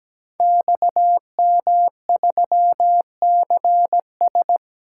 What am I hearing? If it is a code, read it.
Morse code, heard as a keyed tone, about 17 wpm, 700 Hz. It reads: XM3CS
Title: Code Groups - 5 Letter:Numbers - 17wpm